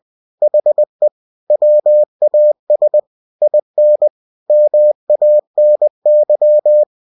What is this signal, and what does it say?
Morse code, heard as a keyed tone, about 20 wpm, 600 Hz. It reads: HE WAS IN MANY
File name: cw-books / winnie-the-pooh / winnie-20wpm-page-0029